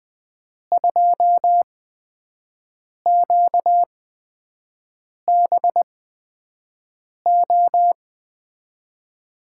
Morse code, heard as a keyed tone, about 20 wpm, spaced as for 5 wpm, 700 Hz. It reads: 2QBO
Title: USA Call Signs - 6 Characters - 8x Character Spacing - 20wpm